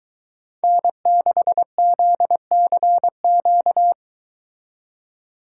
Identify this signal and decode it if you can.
Morse code, heard as a keyed tone, about 22 wpm, 700 Hz. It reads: N6ZCQ